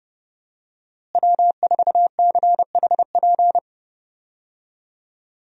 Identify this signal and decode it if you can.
Morse code, heard as a keyed tone, about 30 wpm, 700 Hz. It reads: W4CHP
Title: USA Call Signs - 5 Characters - Repeated 2x - 30wpm